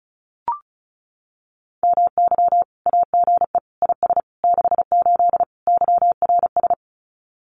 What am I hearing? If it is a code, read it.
Morse code, heard as a keyed tone, about 34 wpm, 700 Hz. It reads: MY AGE IS 68 YRS
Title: Common QSO Phrases Encore 35wpm